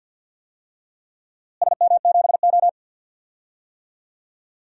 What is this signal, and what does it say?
Morse code, heard as a keyed tone, about 52 wpm, 700 Hz. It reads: SM7O
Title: DXCC-100 Call Signs - 4 Characters - Repeated 2x - 50wpm